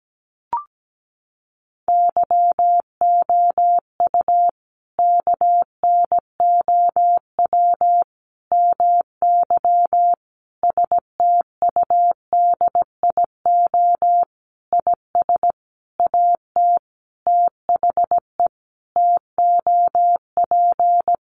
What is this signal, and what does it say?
Morse code, heard as a keyed tone, about 17 wpm, 700 Hz. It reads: YOU KNOW MY STUDIO IS AT THE TOP
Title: Sentences from Top 1200 Words 17wpm